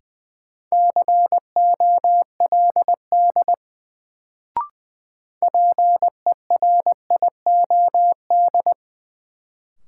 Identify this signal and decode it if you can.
Morse code, heard as a keyed tone, about 20 wpm, 700 Hz. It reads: COLD PERIOD
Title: Sets of 5 Words - New Word Drill - 20wpm